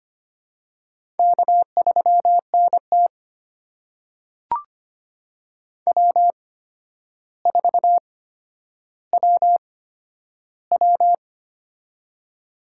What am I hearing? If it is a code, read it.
Morse code, heard as a keyed tone, about 25 wpm, spaced as for 11 wpm, 700 Hz. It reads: K3NT W4WW
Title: USA Call Signs - 4 Characters - 8x Character Spacing - 25wpm